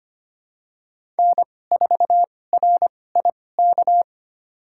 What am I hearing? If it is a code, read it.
Morse code, heard as a keyed tone, about 25 wpm, spaced as for 18 wpm, 700 Hz. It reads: N4RIK
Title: USA Call Signs - 5 Characters - 2x Character Spacing - 25wpm